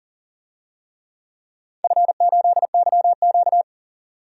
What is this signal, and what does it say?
Morse code, heard as a keyed tone, about 40 wpm, 700 Hz. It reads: F8YQ
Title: DXCC-100 Call Signs - 4 Characters - 40wpm